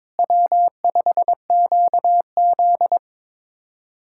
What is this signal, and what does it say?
Morse code, heard as a keyed tone, about 22 wpm, 700 Hz. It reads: W5QZ
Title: USA Callsign DE Callsign - 4 Characters - 22wpm